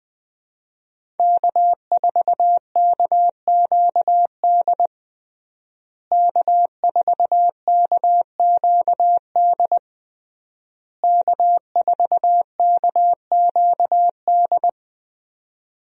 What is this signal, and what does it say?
Morse code, heard as a keyed tone, about 20 wpm, 700 Hz. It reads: K4KQD K4KQD K4KQD
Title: USA Call Signs - 5 Characters - Repeated 3x - 20wpm